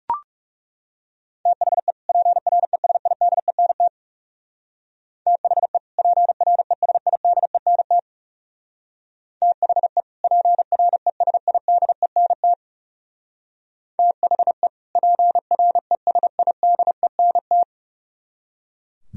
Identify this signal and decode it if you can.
Morse code, heard as a keyed tone, about 35 wpm, 700 Hz. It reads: THE PRESIDENT THE PRESIDENT THE PRESIDENT THE PRESIDENT
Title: Speed-Racing - Sets of 2 Words 30wpm